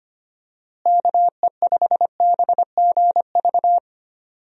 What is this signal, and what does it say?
Morse code, heard as a keyed tone, about 25 wpm, 700 Hz. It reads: KE5BGV
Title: USA Call Signs - 6 Characters - Repeated 3x - 25wpm